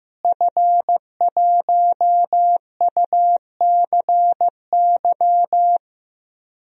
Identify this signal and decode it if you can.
Morse code, heard as a keyed tone, about 15 wpm, 700 Hz. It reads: F1UCY
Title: DXCC-100 Call Signs - 5 Characters - Repeated 3x - 15wpm